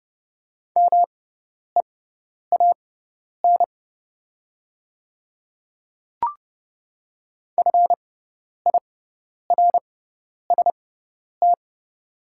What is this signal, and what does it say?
Morse code, heard as a keyed tone, about 30 wpm, spaced as for 10 wpm, 700 Hz. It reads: MEAN FIRST
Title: 6x Character Spacing - Top 200 Words - 30wpm